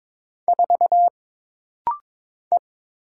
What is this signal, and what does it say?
Morse code, heard as a keyed tone, about 22 wpm, 700 Hz. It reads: 4 E
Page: This rapid-fire timing is geared to push your ICR (Instant Character Recognition) skills.